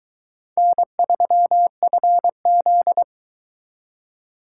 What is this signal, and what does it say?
Morse code, heard as a keyed tone, about 22 wpm, 700 Hz. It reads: N3FZ